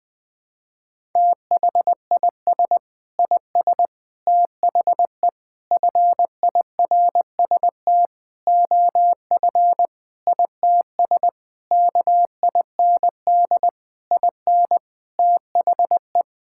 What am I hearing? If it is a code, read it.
Morse code, heard as a keyed tone, about 20 wpm, 700 Hz. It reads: THIS IS THE FIRST OF ITS KIND IN THE
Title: Sentences with 6-Word Phrases - Spoken Beginning - 20wpm